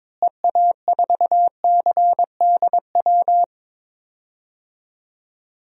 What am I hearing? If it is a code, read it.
Morse code, heard as a keyed tone, about 21 wpm, 700 Hz. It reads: EA4CDW
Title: DXCC-100 Call Signs - 6 Characters - Repeated 3x - 22wpm